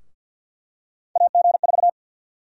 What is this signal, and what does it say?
Morse code, heard as a keyed tone, about 52 wpm, 700 Hz. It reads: AG4
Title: USA Call Sign Prefixes - Repeated 2x - 50wpm